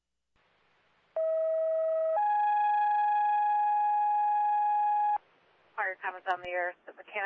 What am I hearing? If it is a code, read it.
Two-tone page, heard as a one-second tone, then a three-second tone, about 640 then 830 Hz, with speech. It is heard from 1.2 to 5.2 s.